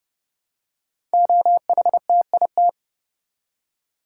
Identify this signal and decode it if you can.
Morse code, heard as a keyed tone, about 30 wpm, 700 Hz. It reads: OHTIT
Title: Code Groups - 5 Letters - 30wpm